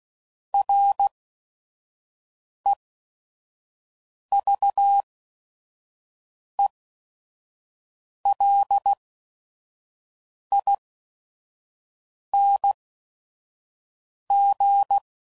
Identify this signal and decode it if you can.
Morse code, heard as a keyed tone, about 16 wpm, spaced as for 5 wpm, 800 Hz. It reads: REVELING
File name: Prefixes / By First Letter / Rs